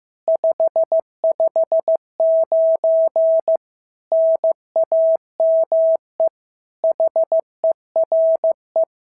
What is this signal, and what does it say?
Morse code, heard as a keyed tone, about 14 wpm, 650 Hz. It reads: 559 NAME HERE
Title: SOTA QSO Practice 15wpm